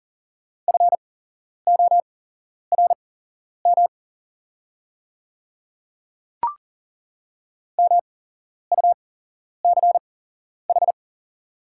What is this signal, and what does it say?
Morse code, heard as a keyed tone, about 40 wpm, spaced as for 11 wpm, 690 Hz. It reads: FORM MUCH
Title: 8x Character Spacing - Top 300 Words - 40wpm